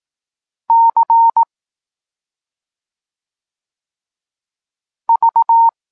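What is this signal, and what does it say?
Morse code, heard as a keyed tone, about 18 wpm, 920 Hz. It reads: C V
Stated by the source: Polish Morse lesson teaching the letter C, from SP9MOA Niepołomicki Klub Krótkofalowców.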